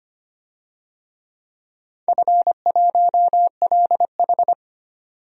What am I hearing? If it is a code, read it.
Morse code, heard as a keyed tone, about 25 wpm, 700 Hz. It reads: F1LH
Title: DXCC-100 Call Signs - 4 Characters - 25wpm